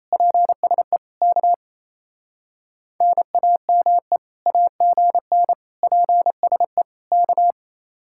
Morse code, heard as a keyed tone, about 34 wpm, 700 Hz. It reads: PSE K NAME AGN PSE K
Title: Speed-Racing Common QSO Phrases - 28wpm